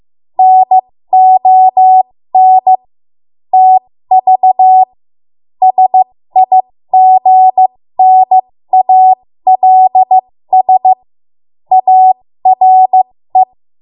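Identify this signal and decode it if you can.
Morse code, heard as a keyed tone, about 15 wpm, spaced as for 13 wpm, 750 Hz. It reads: NON TV SIGNALS ARE
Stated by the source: 13 wpm Morse code practice archive transmission.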